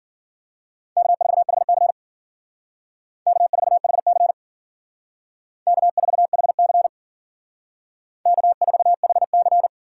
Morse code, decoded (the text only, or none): K4HC K4HC K4HC K4HC